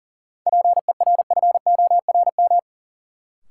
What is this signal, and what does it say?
Morse code, heard as a keyed tone, about 40 wpm, 700 Hz. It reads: PERFORM
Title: Sentences from Top 1000 Words - Review Part 5 - New Word Drill - 40wpm